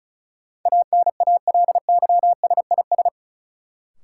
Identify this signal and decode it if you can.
Morse code, heard as a keyed tone, about 36 wpm, 700 Hz. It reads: ANALYSIS
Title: Sentences from Top 1000 Words - Review Part 5 - New Word Drill - 35wpm